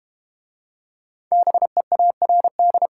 Morse code, heard as a keyed tone, about 32 wpm, 700 Hz. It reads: BEARD